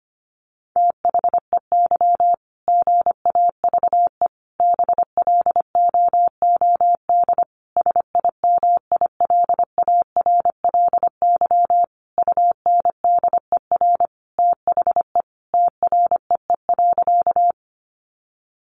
Morse code, heard as a keyed tone, about 25 wpm, 700 Hz. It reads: THEY GAVE BLOOD SIMILARLY UNDER THE TREE.
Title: Sentences from Top 1400 Words 25wpm